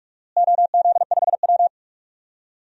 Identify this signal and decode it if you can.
Morse code, heard as a keyed tone, about 46 wpm, 700 Hz. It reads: OZ5W
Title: DXCC-100 Call Signs - 4 Characters - Repeated 2x - 45wpm